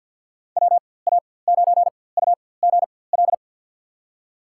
Morse code, heard as a keyed tone, about 46 wpm, spaced as for 23 wpm, 700 Hz. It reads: WA9UGL
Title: USA Call Signs - 6 Characters - 4x Character Spacing - 50wpm